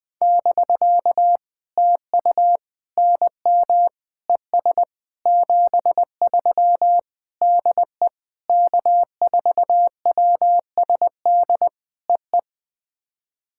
Morse code, heard as a keyed tone, about 20 wpm, 700 Hz. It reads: <BK> TU NM ES 73 DE K4WSD EE
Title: POTA QSO Practice - 20wpm